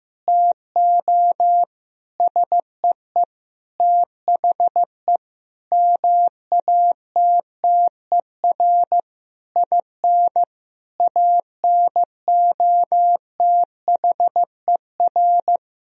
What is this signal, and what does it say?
Morse code, heard as a keyed tone, about 15 wpm, 700 Hz. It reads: TO SEE THE MATTER IN ANOTHER
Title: Sentences from Top 1000 Words - Review Part 1 - 15wpm